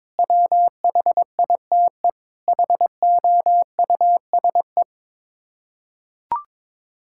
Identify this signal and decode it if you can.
Morse code, heard as a keyed tone, about 22 wpm, 700 Hz. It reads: WHITE HOUSE
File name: practice / 22wpm / Sets of 8 Words - 22wpm